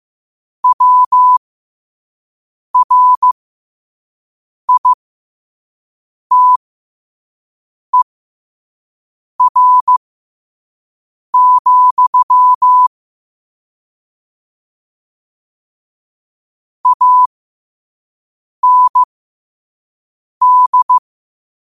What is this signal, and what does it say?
Morse code, heard as a keyed tone, about 15 wpm, spaced as for 5 wpm, 1000 Hz. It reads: WRITER, AND
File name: qotdfiles / 5wpm